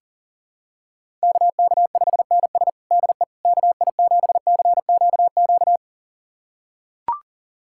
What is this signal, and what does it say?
Morse code, heard as a keyed tone, about 40 wpm, 700 Hz. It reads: KK5NS DE KI7CQQ
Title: USA Callsign DE Callsign 40wpm